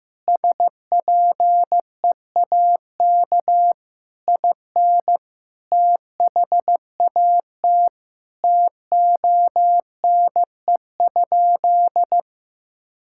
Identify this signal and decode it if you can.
Morse code, heard as a keyed tone, about 16 wpm, 700 Hz. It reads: SPEAK IN THAT TONE?
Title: Sentences from Top 1600 Words 15wpm